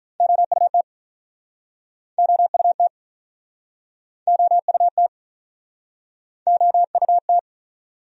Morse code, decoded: OUT OUT OUT OUT

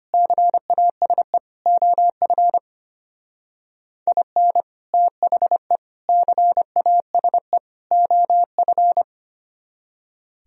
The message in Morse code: CASE OF IN THE CASE OF

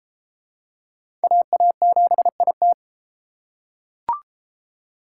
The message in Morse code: AA7IT

